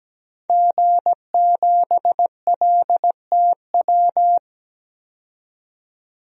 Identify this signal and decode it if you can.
Morse code, heard as a keyed tone, about 17 wpm, 700 Hz. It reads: G7LTW
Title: DXCC-100 Call Signs - 5 Characters - 17wpm